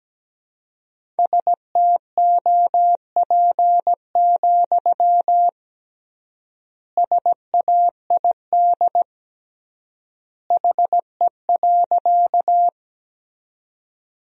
Morse code, heard as a keyed tone, about 17 wpm, 700 Hz. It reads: STOP, SAID HE.
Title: 3x Word-Spacing - Sentences from Top 300 Words - 17wpm